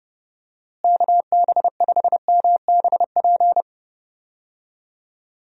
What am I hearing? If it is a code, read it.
Morse code, heard as a keyed tone, about 32 wpm, 700 Hz. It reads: KB5MBP
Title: USA Call Signs - 6 Characters - Repeated 2x - 30wpm